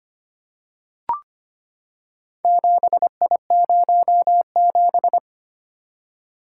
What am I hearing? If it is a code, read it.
Morse code, heard as a keyed tone, about 25 wpm, 700 Hz. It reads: 7I07
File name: practice / 25wpm / Four Letter-Number Combinations - 25wpm